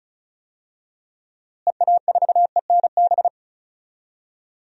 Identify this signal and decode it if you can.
Morse code, heard as a keyed tone, about 36 wpm, 700 Hz. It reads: EA4ENB
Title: DXCC-100 Call Signs - 6 Characters - Repeated 3x - 35wpm